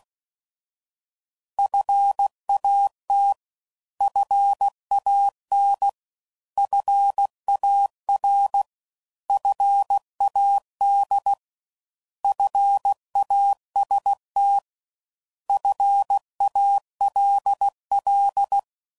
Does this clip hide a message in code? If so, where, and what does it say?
Yes: Morse code from 1.6 to 18.6 s, heard as a keyed tone, about 16 wpm, 800 Hz. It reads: FAT FAN FAR FAD FAST FALL